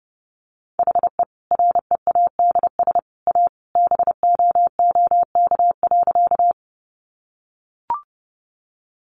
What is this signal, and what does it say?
Morse code, heard as a keyed tone, about 30 wpm, 700 Hz. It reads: HE READS A BOOK.
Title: Additional Easy Phrases 30wpm